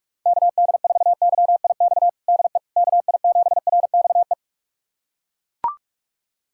Morse code, heard as a keyed tone, about 41 wpm, 700 Hz. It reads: KD4YIX DE KI7RXE
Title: USA Callsign DE Callsign - 6 Characters - 45wpm